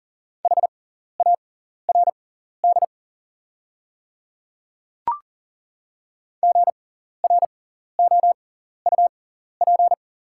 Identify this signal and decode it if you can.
Morse code, heard as a keyed tone, about 38 wpm, spaced as for 14 wpm, 690 Hz. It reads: HARD GROUP
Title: 6x Character Spacing - Top 300 Words - 40wpm